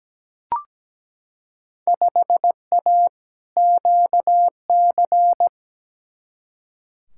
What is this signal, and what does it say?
Morse code, heard as a keyed tone, about 17 wpm, 700 Hz. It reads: 5A QC
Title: Field Day Exchanges Encore - 17wpm